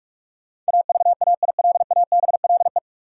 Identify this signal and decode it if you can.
Morse code, heard as a keyed tone, about 46 wpm, 700 Hz. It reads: AVAILABLE
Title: Sentences from Top 600 Words - New Word Drill - 45wpm